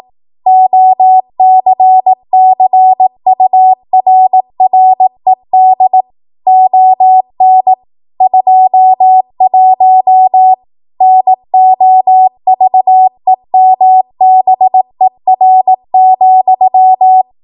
Morse code, heard as a keyed tone, about 18 wpm, 750 Hz. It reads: OCCURRED ON 21 NOVEMBER,
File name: Archive / 18 WPM / 18WPM